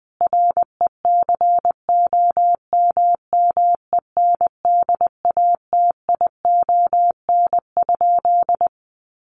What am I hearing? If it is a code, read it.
Morse code, heard as a keyed tone, about 20 wpm, 700 Hz. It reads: RECOMMENDATION?
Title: Sentences from Top 1400 Words 20wpm